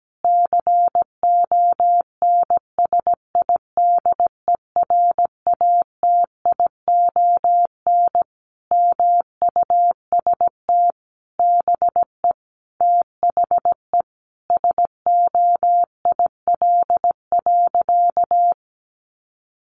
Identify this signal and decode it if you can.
Morse code, heard as a keyed tone, about 17 wpm, 700 Hz. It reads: CONSIDERATION MUST BE THE SOIL.
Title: Sentences from Top 1500 Words 17wpm